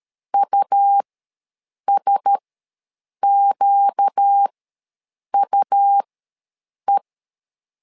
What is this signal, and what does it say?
Morse code, heard as a keyed tone, about 13 wpm, spaced as for 7 wpm, 800 Hz. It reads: USQUE
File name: LEZIONI / 4 Q1S ws 12wpm 1827